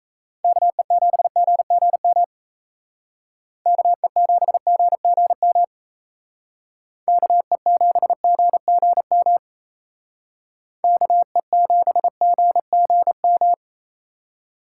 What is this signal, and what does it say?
Morse code, heard as a keyed tone, about 36 wpm, 700 Hz. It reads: KE7GGM KE7GGM KE7GGM KE7GGM